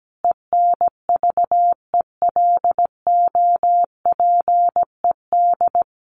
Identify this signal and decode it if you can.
Morse code, heard as a keyed tone, about 17 wpm, 700 Hz. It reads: ENVELOPED
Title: A Sound Of Thunder 17wpm